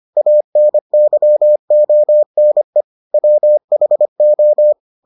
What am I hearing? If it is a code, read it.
Morse code, heard as a keyed tone, about 25 wpm, 600 Hz. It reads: ANYONE WHO